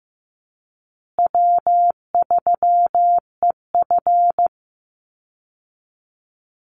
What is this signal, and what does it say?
Morse code, heard as a keyed tone, about 15 wpm, 700 Hz. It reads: W3EF
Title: Call Signs Repeated 3x - 15wpm